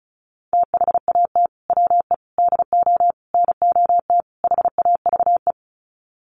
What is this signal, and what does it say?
Morse code, heard as a keyed tone, about 34 wpm, 700 Hz. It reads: THAT WE DO NOT HAVE
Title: Sets of 5 Words Encore 35wpm